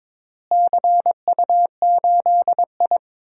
Morse code, heard as a keyed tone, about 22 wpm, 700 Hz. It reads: CU8I
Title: DXCC-100 Call Signs - 4 Characters - Repeated 3x - 22wpm